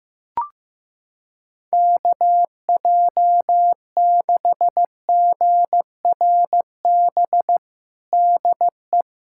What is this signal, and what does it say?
Morse code, heard as a keyed tone, about 16 wpm, 700 Hz. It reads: KJ6GRB DE
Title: USA Callsign DE Callsign - 6 Characters - 15wpm